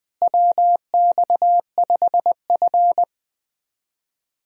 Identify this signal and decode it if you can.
Morse code, heard as a keyed tone, about 21 wpm, 700 Hz. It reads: WX5F
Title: USA Call Signs - 4 Characters - Repeated 3x - 20wpm